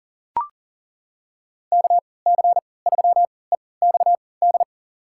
Morse code, heard as a keyed tone, about 40 wpm, spaced as for 23 wpm, 700 Hz. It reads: KC3EXD
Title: USA Call Signs - 6 Characters - 3x Character Spacing - 40wpm